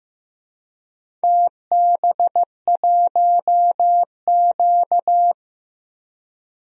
Morse code, heard as a keyed tone, about 15 wpm, 700 Hz. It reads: TB1Q
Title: DXCC-100 Call Signs - 4 Characters - Repeated 3x - 15wpm